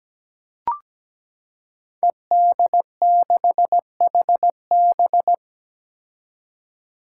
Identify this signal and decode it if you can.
Morse code, heard as a keyed tone, about 17 wpm, 700 Hz. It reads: ED6HB
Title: Five Challenging Letter-Number Combinations 17wpm